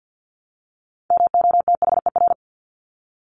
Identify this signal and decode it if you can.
Morse code, heard as a keyed tone, about 50 wpm, 700 Hz. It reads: MOTHER
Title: Top 300 Words Repeated 50wpm